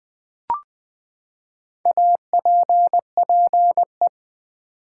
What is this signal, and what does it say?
Morse code, heard as a keyed tone, about 20 wpm, 700 Hz. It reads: APPE
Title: Build A Word - 4 Letters At A Time - 10-Letter Words - 20wpm